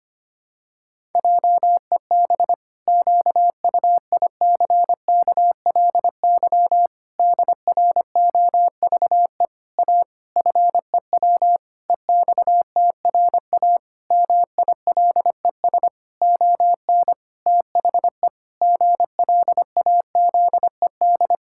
Morse code, heard as a keyed tone, about 25 wpm, 700 Hz. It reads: JEB QUICKLY DROVE A FEW EXTRA MILES ON THE GLAZED
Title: Pangram Challenge v2 - 25wpm